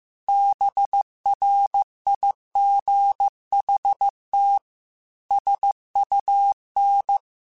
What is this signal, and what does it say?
Morse code, heard as a keyed tone, about 15 wpm, 800 Hz. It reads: BRIGHT SUN
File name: Weather Sets / Weather Sets-mp3s / Weather 5 12 @13